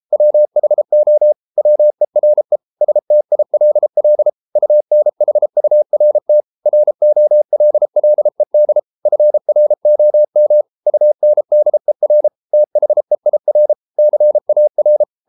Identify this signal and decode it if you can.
Morse code, heard as a keyed tone, about 33 wpm, 600 Hz. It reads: WHO WERE STILL UNHURT ROLLED FROM UNDER THEIR CAR